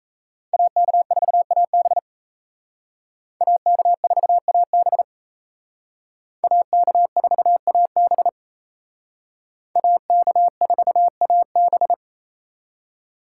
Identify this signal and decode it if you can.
Morse code, heard as a keyed tone, about 36 wpm, 700 Hz. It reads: AK4AB AK4AB AK4AB AK4AB